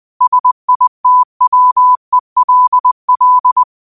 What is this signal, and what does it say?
Morse code, heard as a keyed tone, about 20 wpm, 1000 Hz. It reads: SITWELL